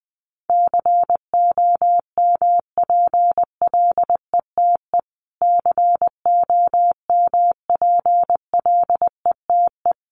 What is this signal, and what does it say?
Morse code, heard as a keyed tone, about 20 wpm, 700 Hz. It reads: COMPLETE COMPLETE
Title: Top 500 Words Repeated 20wpm